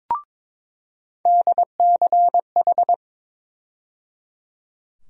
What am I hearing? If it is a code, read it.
Morse code, heard as a keyed tone, about 22 wpm, 700 Hz. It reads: DCH
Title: USA Call Sign Postfixes - 3 Letters 22wpm